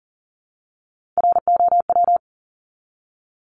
Morse code, heard as a keyed tone, about 40 wpm, 700 Hz. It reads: ROW
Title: Top 1751-2000 Words 40wpm